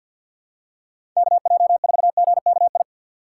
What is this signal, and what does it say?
Morse code, heard as a keyed tone, about 47 wpm, 700 Hz. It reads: KJ4GKI
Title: USA Call Signs - 6 Characters - 6x Character Spacing - 50wpm